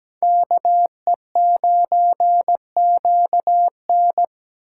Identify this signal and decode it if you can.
Morse code, heard as a keyed tone, about 20 wpm, 700 Hz. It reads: KE9QN
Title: Speed Racing - USA Call Signs 17wpm